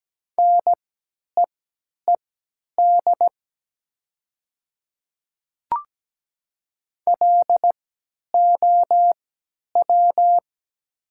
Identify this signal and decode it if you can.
Morse code, heard as a keyed tone, about 17 wpm, spaced as for 10 wpm, 700 Hz. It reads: NEED LOW
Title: 3x Character Spacing - Top 200 Words - 17wpm